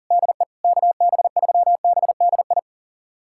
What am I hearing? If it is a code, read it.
Morse code, heard as a keyed tone, about 40 wpm, 700 Hz. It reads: DE KB3BDI
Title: USA Callsign DE Callsign 40wpm